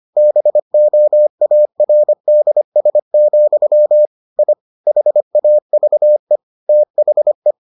Morse code, heard as a keyed tone, about 25 wpm, 600 Hz. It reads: BOARDS, I HAVE THE